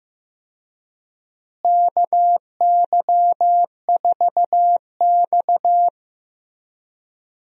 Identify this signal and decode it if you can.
Morse code, heard as a keyed tone, about 15 wpm, 700 Hz. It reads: KY4X
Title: USA Call Signs - 4 Characters - Repeated 3x - 15wpm